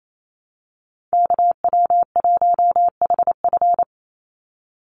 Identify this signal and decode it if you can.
Morse code, heard as a keyed tone, about 28 wpm, 700 Hz. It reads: KW1HF
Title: USA Call Signs 28wpm